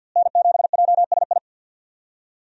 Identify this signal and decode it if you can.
Morse code, heard as a keyed tone, about 52 wpm, 700 Hz. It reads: N7JSI
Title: USA Call Signs - 5 Characters - Repeated 2x - 50wpm